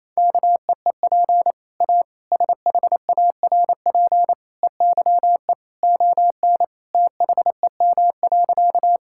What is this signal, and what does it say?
Morse code, heard as a keyed tone, about 27 wpm, 700 Hz. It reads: KEEP A SHARP EYE ON THEM.
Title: Sentences from Top 1600 Words 28wpm